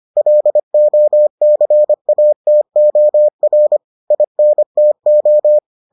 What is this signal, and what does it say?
Morse code, heard as a keyed tone, about 25 wpm, 600 Hz. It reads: LOCATOR INTO